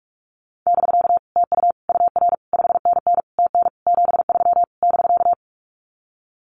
Morse code, heard as a keyed tone, about 46 wpm, 700 Hz. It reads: <BK> TU UR 5NN TN 73 <BK>
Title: Common QSO Phrases Encore 45wpm